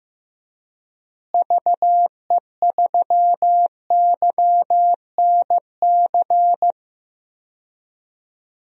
Morse code, heard as a keyed tone, about 15 wpm, 700 Hz. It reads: VE3YNC
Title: DXCC-100 Call Signs - 6 Characters - Repeated 2x - 15wpm